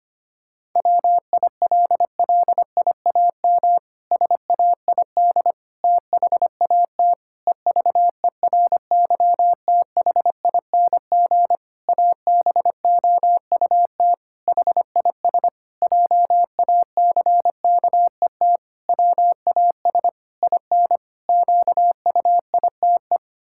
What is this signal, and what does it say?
Morse code, heard as a keyed tone, about 25 wpm, 700 Hz. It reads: WILLIAM SAID THAT EVERYTHING ABOUT HIS JACKET WAS IN QUITE
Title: Pangram Challenge v2 - 25wpm